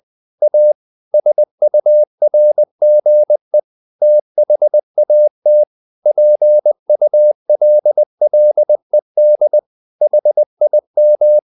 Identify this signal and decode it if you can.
Morse code, heard as a keyed tone, about 20 wpm, 600 Hz. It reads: A SURGE THAT PULLED HIM
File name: the old man and the sea / the old man and the sea part 1 20wpm